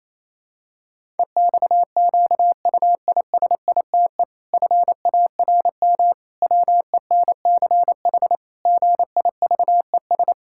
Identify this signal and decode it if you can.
Morse code, heard as a keyed tone, about 28 wpm, 700 Hz. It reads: EXQUISITE FARM WENCH GIVES
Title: Pangram Challenge v2 - 28wpm